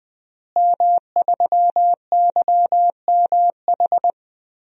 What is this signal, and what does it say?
Morse code, heard as a keyed tone, about 20 wpm, 700 Hz. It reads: M3YMH
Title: DXCC-100 Call Signs - 5 Characters - Repeated 2x - 20wpm